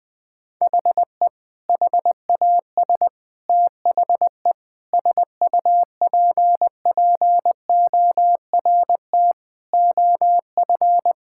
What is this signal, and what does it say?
Morse code, heard as a keyed tone, about 20 wpm, 700 Hz. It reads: HE HAS THE SUPPORT OF
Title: Sentences from Top 1000 Words - Review Part 5 - 20wpm